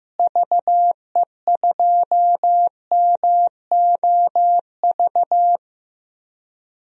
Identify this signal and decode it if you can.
Morse code, heard as a keyed tone, about 15 wpm, 700 Hz. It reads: VE2MOV
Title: DXCC-100 Call Signs - 6 Characters - Repeated 3x - 15wpm